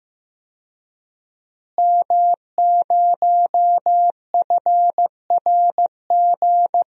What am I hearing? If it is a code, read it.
Morse code, heard as a keyed tone, about 15 wpm, 700 Hz. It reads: M0FRG
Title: DXCC-100 Call Signs - 5 Characters - Repeated 2x - 15wpm